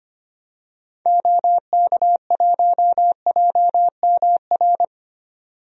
Morse code, heard as a keyed tone, about 25 wpm, 700 Hz. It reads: OK1JMR
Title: DXCC-100 Call Signs - 6 Characters - 25wpm